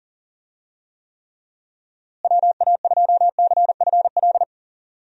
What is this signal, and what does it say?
Morse code, heard as a keyed tone, about 38 wpm, 700 Hz. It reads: WA2CFL
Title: USA Call Signs - 6 Characters - Repeated 3x - 40wpm